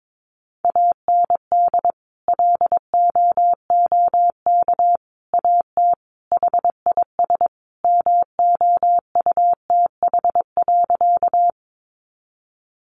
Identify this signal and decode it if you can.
Morse code, heard as a keyed tone, about 22 wpm, 700 Hz. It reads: AND LOOK AT HIS MOUTH.
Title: Sentences from Top 1500 Words 22wpm